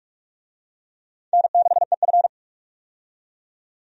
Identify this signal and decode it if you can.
Morse code, heard as a keyed tone, about 46 wpm, 700 Hz. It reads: N6EF
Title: USA Call Signs - 4 Characters - 45wpm